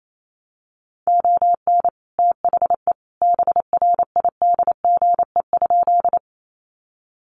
Morse code, heard as a keyed tone, about 28 wpm, 700 Hz. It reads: ON THE BRIDGE?
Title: Sentences from Top 1200 Words 28wpm